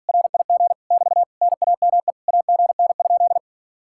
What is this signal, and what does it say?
Morse code, heard as a keyed tone, about 49 wpm, 700 Hz. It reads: RIG <BT> NAME AGN?
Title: Speed-Racing Common QSO Phrases Encore - 40wpm